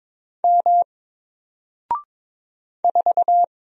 Morse code, heard as a keyed tone, about 22 wpm, 700 Hz. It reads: M 4
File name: Single-Letter-Number-22wpm